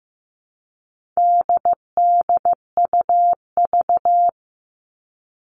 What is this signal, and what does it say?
Morse code, heard as a keyed tone, about 15 wpm, 700 Hz. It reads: DDUV
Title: All Cut Numbers - 4-Digit 15wpm